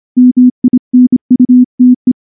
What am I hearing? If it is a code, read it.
Morse code, heard as a keyed tone, about 24 wpm, 250 Hz. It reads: MINUTE
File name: morse-training-25WPM